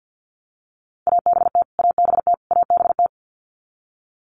Morse code, heard as a keyed tone, about 50 wpm, 700 Hz. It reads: ABT ABT ABT
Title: Abbreviations Repeated 3x - 50wpm